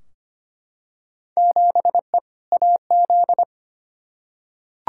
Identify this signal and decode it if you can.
Morse code, heard as a keyed tone, about 25 wpm, 700 Hz. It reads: 7E AZ
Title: Field Day Exchanges Encore - 25wpm